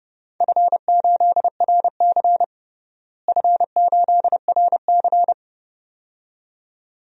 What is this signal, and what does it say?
Morse code, heard as a keyed tone, about 31 wpm, 700 Hz. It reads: F8RC F8RC
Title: DXCC-100 Call Signs - 4 Characters - Repeated 2x - 30wpm